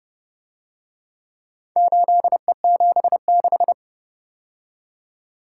Morse code, heard as a keyed tone, about 30 wpm, 700 Hz. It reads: 8E76